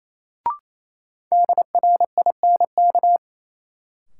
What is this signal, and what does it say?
Morse code, heard as a keyed tone, about 28 wpm, 700 Hz. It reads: DRINK
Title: Sentences from Top 1500 Words - New Word Drill Alt - 28wpm